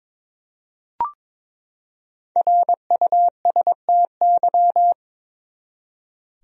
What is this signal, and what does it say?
Morse code, heard as a keyed tone, about 21 wpm, 700 Hz. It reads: RUSTY
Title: USA Common Names - Max 5 Letters - 22wpm